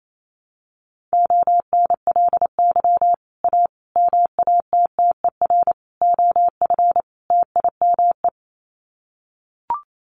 Morse code, heard as a keyed tone, about 28 wpm, 700 Hz. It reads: ONLY A MATTER OF TIME
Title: Sets of 5 Words Encore 28wpm